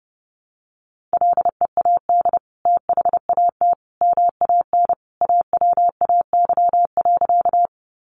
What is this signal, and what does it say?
Morse code, heard as a keyed tone, about 30 wpm, 700 Hz. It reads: LEAD THAT MAN AWAY.
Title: Sentences from Top 600 Words 30wpm